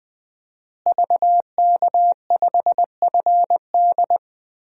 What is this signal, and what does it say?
Morse code, heard as a keyed tone, about 20 wpm, 700 Hz. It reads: VK5FD
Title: DXCC-100 Call Signs - 5 Characters - Repeated 3x - 20wpm